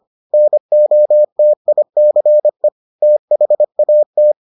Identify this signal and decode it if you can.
Morse code, heard as a keyed tone, about 25 wpm, 600 Hz. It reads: NOTICE THAT